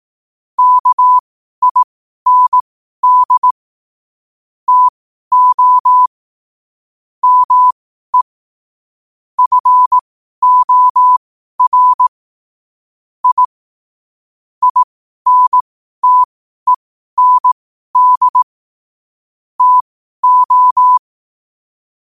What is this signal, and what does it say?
Morse code, heard as a keyed tone, about 18 wpm, spaced as for 12 wpm, 1000 Hz. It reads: KIND TO ME FOR I INTEND TO
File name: qotdfiles / 12wpm